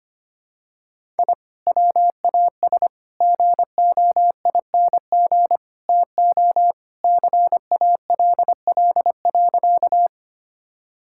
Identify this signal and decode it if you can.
Morse code, heard as a keyed tone, about 25 wpm, 700 Hz. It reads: I WAS GOING TO CALL.
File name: practice / 25wpm / Sentences with 4-Word Phrases - 25wpm